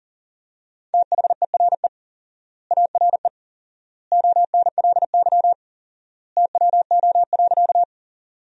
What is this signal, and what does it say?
Morse code, heard as a keyed tone, about 39 wpm, 700 Hz. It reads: THERE ARE ONLY TWO.
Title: 4x Word-Spacing - Sentences from Top 200 Words - 40wpm